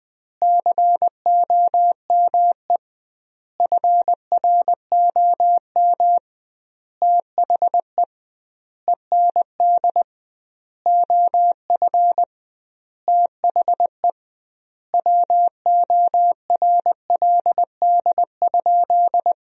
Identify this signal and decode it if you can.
Morse code, heard as a keyed tone, about 20 wpm, 700 Hz. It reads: COME FROM THE END OF THE WORLD?
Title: 2x Word-Spacing - Sentences from Top 200 Words - 20wpm